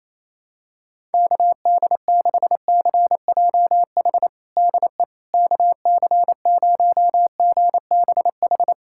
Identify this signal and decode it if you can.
Morse code, heard as a keyed tone, about 28 wpm, 700 Hz. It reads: KD6CJH DE KC0GBH
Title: USA Callsign DE Callsign 28wpm